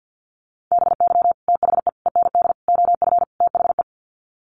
Morse code, heard as a keyed tone, about 50 wpm, 700 Hz. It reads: BY THE END OF THE
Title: Sets of 5 Words 50wpm